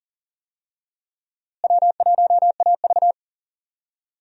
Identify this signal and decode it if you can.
Morse code, heard as a keyed tone, about 40 wpm, 700 Hz. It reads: W1AV
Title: USA Call Signs - 4 Characters - Repeated 2x - 40wpm